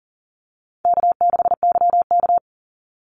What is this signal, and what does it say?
Morse code, heard as a keyed tone, about 40 wpm, 700 Hz. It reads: K6YK
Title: NA SOTA Chaser Callsigns 40wpm